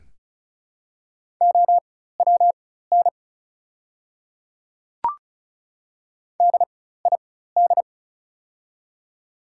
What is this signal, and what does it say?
Morse code, heard as a keyed tone, about 34 wpm, spaced as for 16 wpm, 700 Hz. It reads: OWN DID
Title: 4x Character Spacing - Top 300 Words - 35wpm